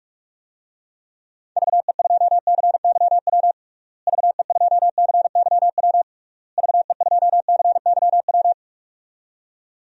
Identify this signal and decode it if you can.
Morse code, heard as a keyed tone, about 41 wpm, 700 Hz. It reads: VE2CYW VE2CYW VE2CYW
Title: DXCC-100 Call Signs - 6 Characters - Repeated 3x - 45wpm